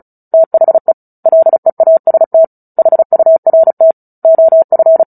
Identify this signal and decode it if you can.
Morse code, heard as a keyed tone, about 34 wpm, 650 Hz. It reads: THE LEAST HURT OF
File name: Book Two / 09-Chapter 9 - 35 WPM